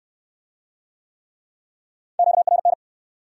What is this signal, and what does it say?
Morse code, heard as a keyed tone, about 68 wpm, 700 Hz. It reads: QRN